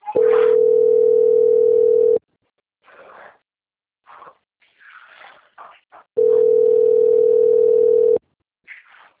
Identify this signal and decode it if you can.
Ringback tone (North America), 440 + 480 Hz.
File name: out- - - - - .967